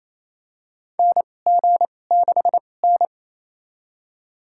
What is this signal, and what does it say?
Morse code, heard as a keyed tone, about 27 wpm, spaced as for 20 wpm, 690 Hz. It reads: NG6N
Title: USA Call Signs - 4 Characters - 2x Character Spacing - 28wpm